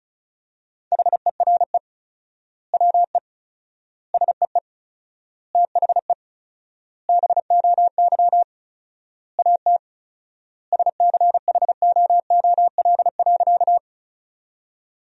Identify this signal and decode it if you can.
Morse code, heard as a keyed tone, about 34 wpm, 700 Hz. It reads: HERE WE SEE THE BOY AT SCHOOL.
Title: 4x Word-Spacing - Sentences from Top 300 Words - 35wpm